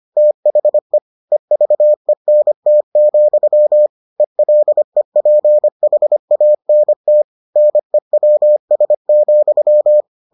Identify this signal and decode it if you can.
Morse code, heard as a keyed tone, about 25 wpm, 600 Hz. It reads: THE EVENT, ELEPHANT NEWS,